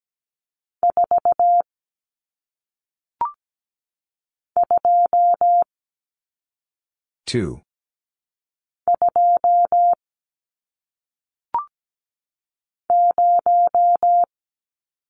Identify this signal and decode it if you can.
Morse code, heard as a keyed tone, about 17 wpm, 700 Hz. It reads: 4 2 2 0
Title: Numbers - 1-Digit 17wpm